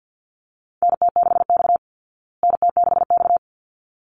Morse code, heard as a keyed tone, about 48 wpm, 700 Hz. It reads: NT6X NT6X
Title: Call Signs Repeated 3x - 50wpm